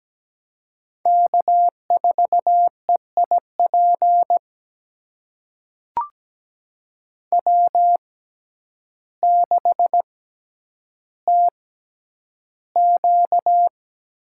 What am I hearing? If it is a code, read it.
Morse code, heard as a keyed tone, about 17 wpm, spaced as for 9 wpm, 700 Hz. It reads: K4EIP W6TQ
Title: USA Call Signs - 5 Characters - 6x Character Spacing - 17wpm